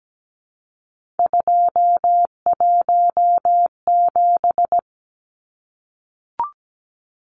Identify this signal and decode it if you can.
Morse code, heard as a keyed tone, about 17 wpm, 700 Hz. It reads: 217
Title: Numbers - 3-Digit 17wpm